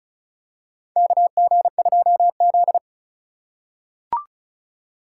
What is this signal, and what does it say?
Morse code, heard as a keyed tone, about 36 wpm, 700 Hz. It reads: KG2Z
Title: USA Call Signs - 2x Character Spacing - 35wpm